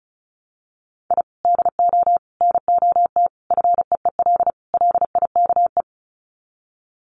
Morse code, heard as a keyed tone, about 34 wpm, 700 Hz. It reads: I DO NOT FEEL LIKE
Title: Sets of 5 Words Encore 35wpm